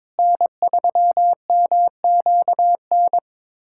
Morse code, heard as a keyed tone, about 22 wpm, 700 Hz. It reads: N3MQN
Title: USA Callsign DE Callsign 22wpm